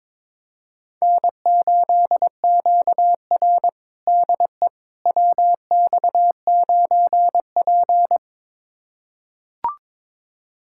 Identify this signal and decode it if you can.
Morse code, heard as a keyed tone, about 22 wpm, 700 Hz. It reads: N8QR DE WX9P
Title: USA Callsign DE Callsign - 4 Characters - 22wpm